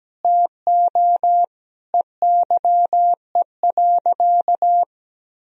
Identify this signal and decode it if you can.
Morse code, heard as a keyed tone, about 17 wpm, 700 Hz. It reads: TO EYE.
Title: Sentences from Top 1000 Words - Review Part 2 - 17wpm